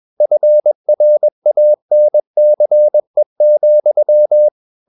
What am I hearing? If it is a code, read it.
Morse code, heard as a keyed tone, about 21 wpm, 600 Hz. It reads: FRANCE,
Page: CW med Gnister